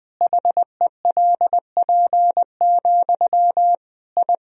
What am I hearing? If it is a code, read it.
Morse code, heard as a keyed tone, about 20 wpm, 700 Hz. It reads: HELP, I
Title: Sentences from Top 1000 Words - Review Part 1 - 20wpm